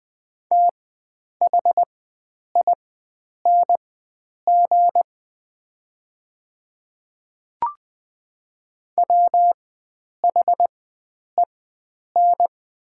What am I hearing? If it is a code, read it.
Morse code, heard as a keyed tone, about 20 wpm, spaced as for 9 wpm, 700 Hz. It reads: THING WHEN
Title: 4x Character Spacing - Top 300 Words - 20wpm